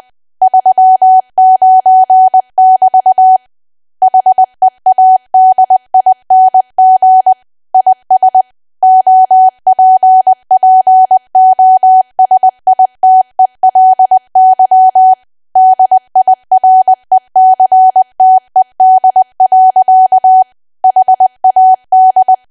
Morse code, heard as a keyed tone, about 20 wpm, 750 Hz. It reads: 39 <BT> HEADING IS OPPOSITELY DIRECTED. HAD
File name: cwa / W1AW-Level3 / 20WPM